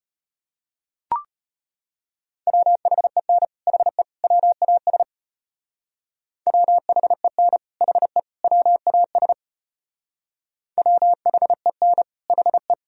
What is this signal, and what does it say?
Morse code, heard as a keyed tone, about 32 wpm, 700 Hz. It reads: WHEN HE WAS WHEN HE WAS WHEN HE